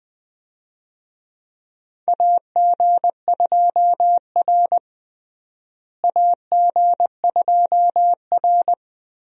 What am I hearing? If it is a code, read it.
Morse code, heard as a keyed tone, about 20 wpm, 700 Hz. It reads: AG2R AG2R
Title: USA Call Signs - 4 Characters - Repeated 3x - 20wpm